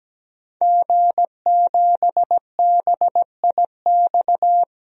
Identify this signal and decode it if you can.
Morse code, heard as a keyed tone, about 17 wpm, 700 Hz. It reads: G7BIX
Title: DXCC-100 Call Signs - 5 Characters - 17wpm